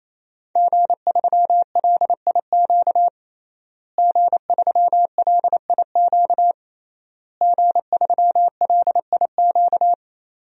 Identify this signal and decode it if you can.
Morse code, heard as a keyed tone, about 27 wpm, 700 Hz. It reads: G3LIQ G3LIQ G3LIQ
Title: DXCC-100 Call Signs - 5 Characters - Repeated 3x - 28wpm